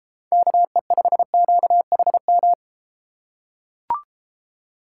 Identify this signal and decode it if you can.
Morse code, heard as a keyed tone, about 33 wpm, 700 Hz. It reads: KE5QHM